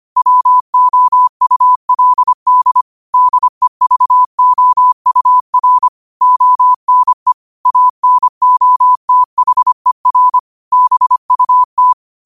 Morse code, heard as a keyed tone, about 25 wpm, 1000 Hz. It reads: WOULD DEVOUR ONE ANOTHER BUT